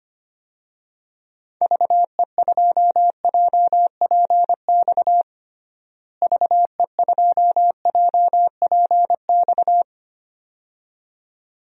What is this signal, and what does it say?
Morse code, heard as a keyed tone, about 25 wpm, 700 Hz. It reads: VE2JPX VE2JPX
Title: DXCC-100 Call Signs - 6 Characters - Repeated 2x - 25wpm